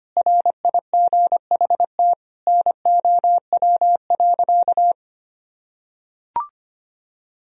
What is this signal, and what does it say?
Morse code, heard as a keyed tone, about 25 wpm, 700 Hz. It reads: RIGHT NOW.
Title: Sentences from Top 1000 Words - Review Part 4 - 25wpm